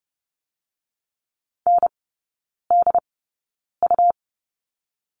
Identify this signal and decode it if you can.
Morse code, heard as a keyed tone, about 30 wpm, spaced as for 9 wpm, 700 Hz. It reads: NDU
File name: Lesson 17e - Letter u - Three Characters 30wpm